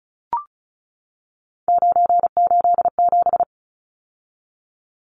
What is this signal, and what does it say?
Morse code, heard as a keyed tone, about 34 wpm, 700 Hz. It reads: 987
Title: Numbers - 3-Digit 35wpm